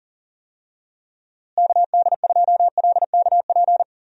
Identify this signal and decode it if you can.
Morse code, heard as a keyed tone, about 39 wpm, 700 Hz. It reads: KD2LKP
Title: USA Call Signs - 6 Characters - Repeated 3x - 40wpm